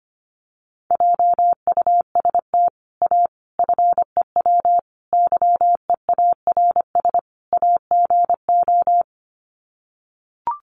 Morse code, heard as a keyed tone, about 25 wpm, 700 Hz. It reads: JUST A FEW YEARS AGO
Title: Sets of 5 Words Encore 25wpm